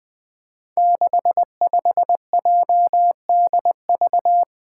Morse code, heard as a keyed tone, about 20 wpm, 700 Hz. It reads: 65JDV
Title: Five Challenging Letter-Number Combinations 20wpm